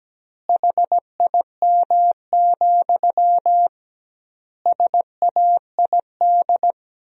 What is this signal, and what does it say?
Morse code, heard as a keyed tone, about 17 wpm, 700 Hz. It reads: HIM, SAID
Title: 2x Word-Spacing - Sentences from Top 400 Words - 17wpm